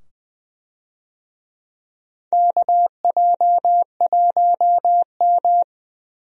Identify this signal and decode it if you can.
Morse code, heard as a keyed tone, about 20 wpm, 700 Hz. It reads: KJ1M
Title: USA Call Signs - 4 Characters - 20wpm